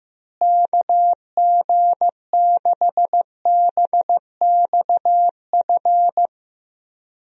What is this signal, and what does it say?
Morse code, heard as a keyed tone, about 15 wpm, 700 Hz. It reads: KG6BXF
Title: USA Call Signs - 6 Characters - Repeated 3x - 15wpm